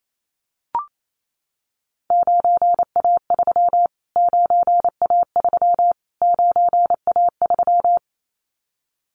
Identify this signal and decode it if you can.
Morse code, heard as a keyed tone, about 28 wpm, 700 Hz. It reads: 9A3 9A3 9A3
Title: Callsign Prefixes for Top 25 DXCC - Repeated 3x - 28wpm